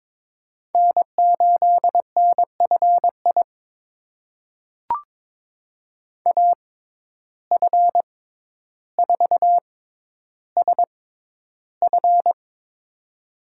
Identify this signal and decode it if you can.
Morse code, heard as a keyed tone, about 22 wpm, spaced as for 11 wpm, 700 Hz. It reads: N8NFI AF4SF
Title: USA Call Signs - 5 Characters - 6x Character Spacing - 22wpm